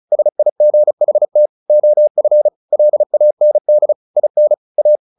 Morse code, heard as a keyed tone, about 34 wpm, 600 Hz. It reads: SIGHT OF LAND IN A
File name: the old man and the sea / the old man and the sea 1 35wpm